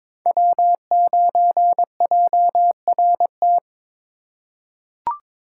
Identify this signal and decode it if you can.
Morse code, heard as a keyed tone, about 22 wpm, 700 Hz. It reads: W9JRT
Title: USA Call Signs - 4x Character Spacing - 22wpm